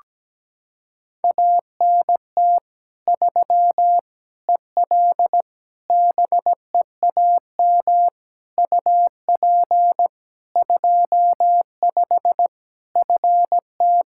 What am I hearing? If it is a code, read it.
Morse code, heard as a keyed tone, about 17 wpm, 700 Hz. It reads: ANT 3 EL BEAM UP 25 FT
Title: K1USN SST QSO Practice 17wpm